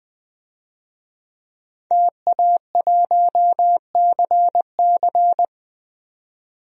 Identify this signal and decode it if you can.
Morse code, heard as a keyed tone, about 20 wpm, 700 Hz. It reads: TA1CC
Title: DXCC-100 Call Signs - 5 Characters - 20wpm